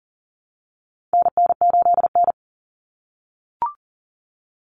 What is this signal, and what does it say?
Morse code, heard as a keyed tone, about 40 wpm, 700 Hz. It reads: NN8N
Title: Common Cut Numbers - 4-Digit 40wpm